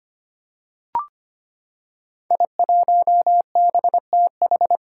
Morse code, heard as a keyed tone, about 25 wpm, 700 Hz. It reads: I1BTH
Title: Five Challenging Letter-Number Combinations 25wpm